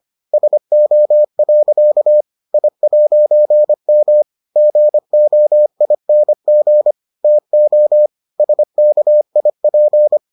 Morse code, heard as a keyed tone, about 25 wpm, 600 Hz. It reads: SO. I'M GOING TO SKIP